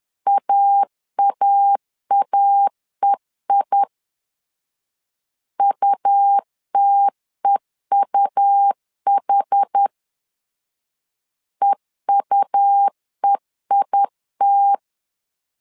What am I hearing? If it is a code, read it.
Morse code, heard as a keyed tone, about 10 wpm, 800 Hz. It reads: AAAEI UTEUH EUEIT
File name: LEZIONI / 2 UHDN / 2 UH 10wpm 1714